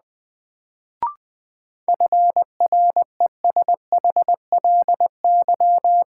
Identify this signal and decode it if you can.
Morse code, heard as a keyed tone, about 20 wpm, 700 Hz. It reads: FRESHLY
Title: Sets of 5 Words Encore - New Word Drill - 20wpm